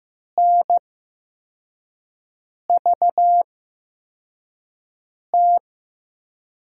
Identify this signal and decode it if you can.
Morse code, heard as a keyed tone, about 15 wpm, spaced as for 4 wpm, 700 Hz. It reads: NVT
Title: USA Call Signs - 5 Characters - 8x Character Spacing - 15wpm